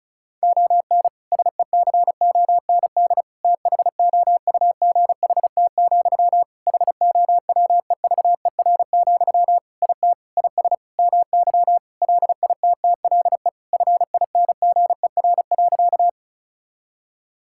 Morse code, heard as a keyed tone, about 34 wpm, 700 Hz. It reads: ON SECOND THOUGHT, HOWEVER, IT IS MY LITTLE FINGER.
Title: Sentences from Top 1600 Words 35wpm